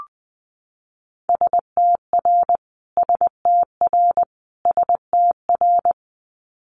Morse code, heard as a keyed tone, about 19 wpm, 700 Hz. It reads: STR STR STR
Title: Multi-letter Phonemes Repeated 3x 20wpm